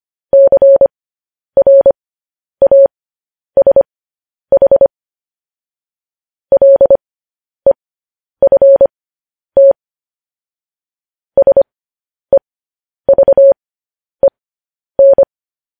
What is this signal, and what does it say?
Morse code, heard as a keyed tone, about 25 wpm, spaced as for 10 wpm, 550 Hz. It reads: CRASH LEFT SEVEN
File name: podcast / News headlines-1x-25x10